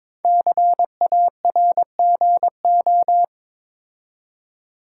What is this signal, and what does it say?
Morse code, heard as a keyed tone, about 22 wpm, 700 Hz. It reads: CARGO
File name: 5 Letter Words - First 3 Letters Given - 22wpm